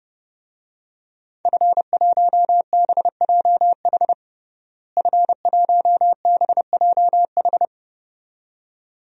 Morse code, heard as a keyed tone, about 30 wpm, 700 Hz. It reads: F1BJH F1BJH
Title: DXCC-100 Call Signs - 5 Characters - Repeated 2x - 30wpm